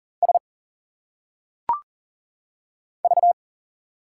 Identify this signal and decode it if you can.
Morse code, heard as a keyed tone, about 40 wpm, 700 Hz. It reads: S V
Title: Single Letters - Phonetic Pronunciation 40wpm